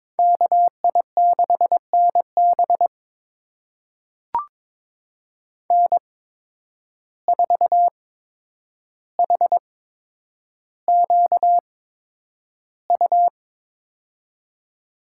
Morse code, heard as a keyed tone, about 21 wpm, spaced as for 9 wpm, 700 Hz. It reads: KI6NB N4HQU
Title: USA Call Signs - 5 Characters - 8x Character Spacing - 22wpm